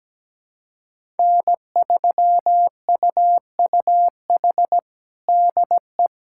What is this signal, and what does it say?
Morse code, heard as a keyed tone, about 17 wpm, 700 Hz. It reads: N3UUH DE
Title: USA Callsign DE Callsign - 5 Characters - 17wpm